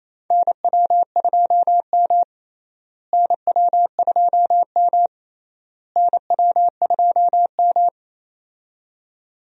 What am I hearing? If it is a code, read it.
Morse code, heard as a keyed tone, about 28 wpm, 700 Hz. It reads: NW2M NW2M NW2M
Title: USA Call Signs - 4 Characters - Repeated 3x - 28wpm